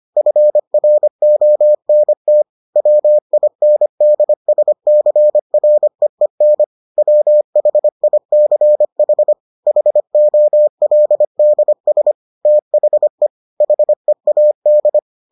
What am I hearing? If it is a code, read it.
Morse code, heard as a keyed tone, about 25 wpm, 600 Hz. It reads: FRONT WINDSCREEN WHICH HOLDS THE HEAD